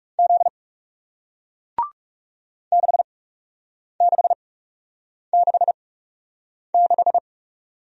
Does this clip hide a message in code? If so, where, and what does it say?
Yes: Morse code from 0.2 to 7.2 s, heard as a keyed tone, about 41 wpm, spaced as for 8 wpm, 700 Hz. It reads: Z 6666